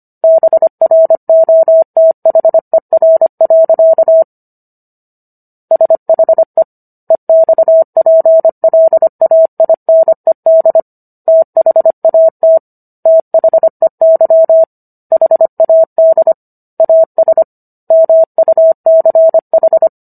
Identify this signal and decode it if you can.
Morse code, heard as a keyed tone, about 25 wpm, 650 Hz. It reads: BROTHER. SHE EXPLAINED THAT THEY HAD AS MUCH